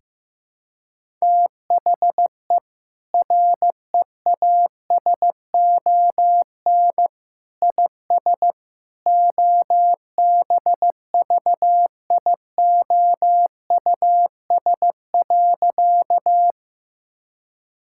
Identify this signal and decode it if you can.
Morse code, heard as a keyed tone, about 15 wpm, 700 Hz. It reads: THE REASON IS OBVIOUS.
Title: Sentences from Top 1600 Words 15wpm